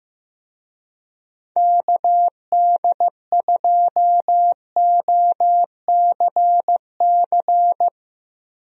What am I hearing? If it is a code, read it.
Morse code, heard as a keyed tone, about 15 wpm, 700 Hz. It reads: KD2OCC
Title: USA Call Signs - 6 Characters - Repeated 2x - 15wpm